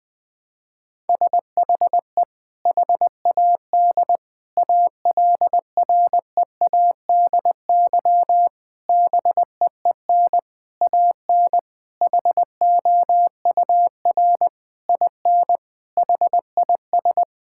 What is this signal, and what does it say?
Morse code, heard as a keyed tone, about 20 wpm, 700 Hz. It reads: SHE HAD ALREADY BEEN AN HOUR IN HIS
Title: Sentences from Top 1000 Words - Review Part 6 - 20wpm